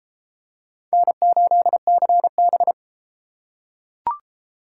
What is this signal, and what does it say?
Morse code, heard as a keyed tone, about 32 wpm, 700 Hz. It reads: N8CB